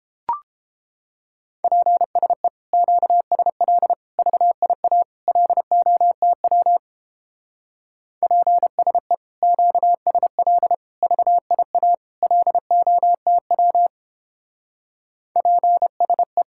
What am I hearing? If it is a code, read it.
Morse code, heard as a keyed tone, about 31 wpm, 700 Hz. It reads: PSE QSL VIA LOTW PSE QSL VIA LOTW PSE